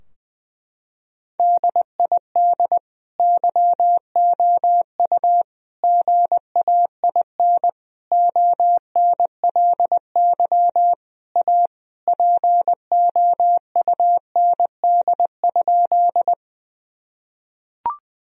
Morse code, heard as a keyed tone, about 20 wpm, 700 Hz. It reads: DID YOU GAIN ONLY A POUND?
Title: Sentences from Top 1000 Words - Review Part 2 - 20wpm